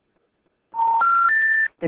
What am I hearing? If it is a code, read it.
Special information tone.